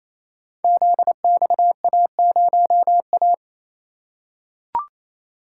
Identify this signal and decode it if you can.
Morse code, heard as a keyed tone, about 28 wpm, 700 Hz. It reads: ZXA0A
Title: Code Groups - 5 Letter:Numbers - 28wpm